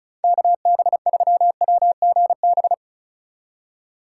Morse code, heard as a keyed tone, about 36 wpm, 700 Hz. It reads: KB3WGB
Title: USA Call Signs - 6 Characters - 3x Character Spacing - 35wpm